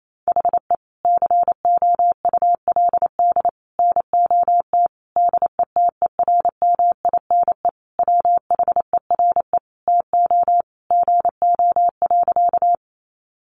Morse code, heard as a keyed tone, about 28 wpm, 700 Hz. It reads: HE COULD NOT DETERMINE WHERE TO GO.
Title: Sentences from Top 700 Words 28wpm